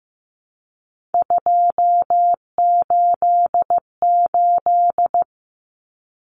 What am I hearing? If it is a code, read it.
Morse code, heard as a keyed tone, about 15 wpm, 700 Hz. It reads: 288
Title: Common Cut Numbers - 3-Digit 15wpm